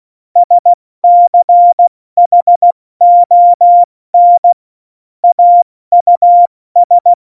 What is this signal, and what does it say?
Morse code, heard as a keyed tone, about 16 wpm, 700 Hz. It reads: SCHON AUS
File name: Morsekurs / Witze / Witz 13. Oktober 2020, 70 F80